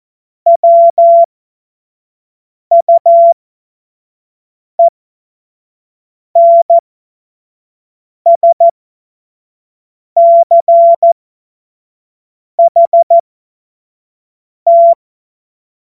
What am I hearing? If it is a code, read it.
Morse code, heard as a keyed tone, about 14 wpm, spaced as for 5 wpm, 680 Hz. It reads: WUENSCHT
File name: Morsekurs / Witze / Witz 06. Dezember 2022 25 F70